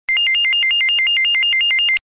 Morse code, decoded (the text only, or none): none